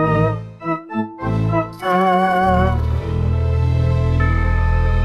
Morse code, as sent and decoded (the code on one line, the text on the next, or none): none